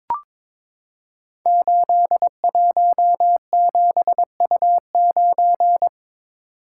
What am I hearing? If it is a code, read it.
Morse code, heard as a keyed tone, about 22 wpm, 700 Hz. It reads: 817U9
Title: Code Groups - 5 Letter:Numbers - 22wpm